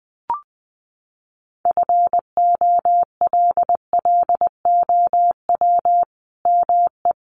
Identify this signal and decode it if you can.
Morse code, heard as a keyed tone, about 20 wpm, 700 Hz. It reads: FOLLOW ME
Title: Sentences from Top 1400 Words 20wpm